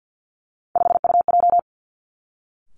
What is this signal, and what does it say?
Morse code, heard as a keyed tone, about 50 wpm, 700 Hz. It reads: 5UJ
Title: Three Challenging Letter-Number Combinations 50wpm